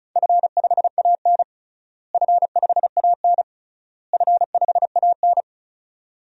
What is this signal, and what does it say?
Morse code, heard as a keyed tone, about 34 wpm, 700 Hz. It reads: F5AN F5AN F5AN
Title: DXCC-100 Call Signs - 4 Characters - Repeated 3x - 35wpm